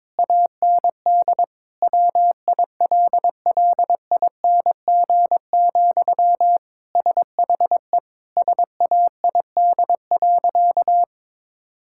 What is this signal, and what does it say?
Morse code, heard as a keyed tone, about 22 wpm, 700 Hz. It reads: AND WILLING, SHE SAID.
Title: Sentences from Top 1600 Words 22wpm